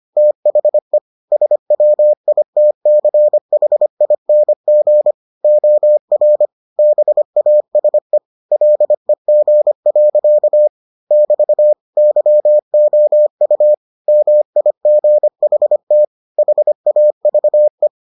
Morse code, heard as a keyed tone, about 25 wpm, 600 Hz. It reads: THE SWITCHING OR BASE LEG. <BT> YOU MIGHT HAVE